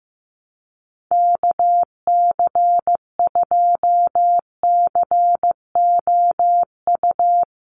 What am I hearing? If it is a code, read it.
Morse code, heard as a keyed tone, about 15 wpm, 700 Hz. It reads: KC2COU
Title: USA Call Signs 15wpm